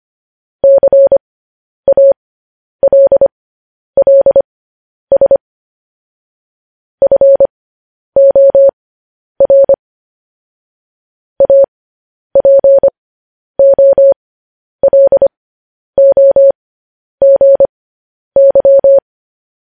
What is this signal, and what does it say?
Morse code, heard as a keyed tone, about 25 wpm, spaced as for 10 wpm, 550 Hz. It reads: CALLS FOR APOLOGY
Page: news headlines in morse code at 10 wpm